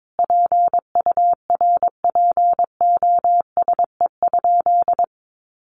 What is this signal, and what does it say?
Morse code, heard as a keyed tone, about 22 wpm, 700 Hz. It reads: PURPOSE?
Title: Sentences from Top 1500 Words 22wpm